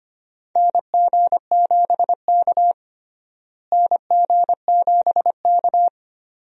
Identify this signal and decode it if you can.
Morse code, heard as a keyed tone, about 25 wpm, 700 Hz. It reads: NG7K NG7K
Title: USA Call Signs - 4 Characters - Repeated 2x - 25wpm